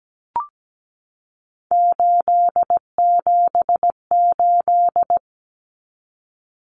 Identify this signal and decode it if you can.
Morse code, heard as a keyed tone, about 17 wpm, 700 Hz. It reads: 878
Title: Common Cut Numbers - 3-Digit 17wpm